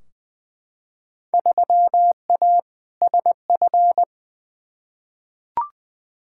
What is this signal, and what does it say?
Morse code, heard as a keyed tone, about 20 wpm, 700 Hz. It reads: 3A SF
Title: Field Day Exchanges Encore - 20wpm